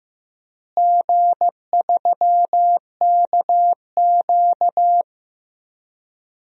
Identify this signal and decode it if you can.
Morse code, heard as a keyed tone, about 15 wpm, 700 Hz. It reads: G3KQ
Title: DXCC-100 Call Signs - 4 Characters - Repeated 2x - 15wpm